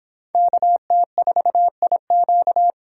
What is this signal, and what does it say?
Morse code, heard as a keyed tone, about 26 wpm, 700 Hz. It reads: KT4IQ